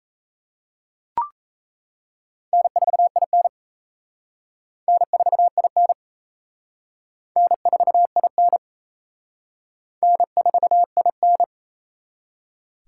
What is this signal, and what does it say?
Morse code, heard as a keyed tone, about 36 wpm, 700 Hz. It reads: N4IN N4IN N4IN N4IN